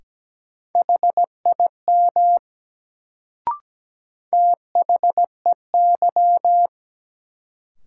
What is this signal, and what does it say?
Morse code, heard as a keyed tone, about 17 wpm, 700 Hz. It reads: HIM THEY
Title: Sentences from Top 100 Words - New Word Drill - 17wpm